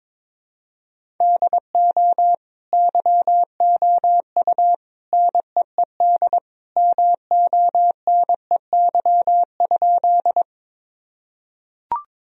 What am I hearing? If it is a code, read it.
Morse code, heard as a keyed tone, about 22 wpm, 700 Hz. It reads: DO YOU NEED MONEY?
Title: Sentences from Top 1000 Words - Review Part 1 - 22wpm